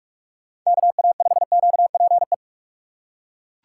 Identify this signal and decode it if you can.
Morse code, heard as a keyed tone, about 40 wpm, 700 Hz. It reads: KA5QPE